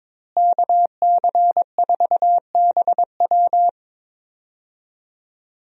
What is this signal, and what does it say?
Morse code, heard as a keyed tone, about 21 wpm, 700 Hz. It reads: KC4BW
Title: USA Call Signs - 5 Characters - Repeated 3x - 22wpm